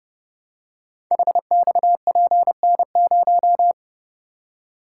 Morse code, heard as a keyed tone, about 28 wpm, 700 Hz. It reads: HXPN0
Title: Code Groups - 5 Letter:Numbers - 30wpm